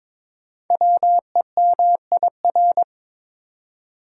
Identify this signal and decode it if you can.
Morse code, heard as a keyed tone, about 23 wpm, 700 Hz. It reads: WEMIR
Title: Code Groups - 5 Letters - 22wpm